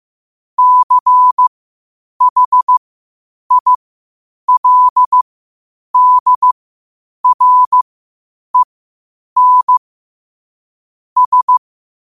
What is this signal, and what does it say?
Morse code, heard as a keyed tone, about 15 wpm, spaced as for 8 wpm, 1000 Hz. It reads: CHILDREN S
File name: 8wpm